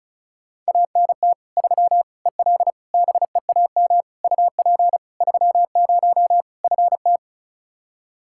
Morse code, heard as a keyed tone, about 36 wpm, 700 Hz. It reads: ANT 3 EL BEAM UP 30 FT
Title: Practice QSO - Exchange 2 - Info - 35wpm